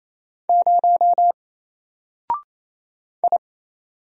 Morse code, heard as a keyed tone, about 28 wpm, 700 Hz. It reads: 0 I
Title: Single Character (Letter:Number:Punctuation:Prosign) 28wpm